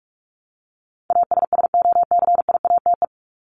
Morse code, heard as a keyed tone, about 44 wpm, 700 Hz. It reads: ASSOCIATE
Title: Top 1751-2000 Words 45wpm